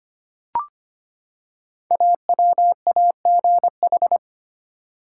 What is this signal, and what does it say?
Morse code, heard as a keyed tone, about 25 wpm, 700 Hz. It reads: AWAGH
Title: Code Groups - 5 Letter:Numbers - 25wpm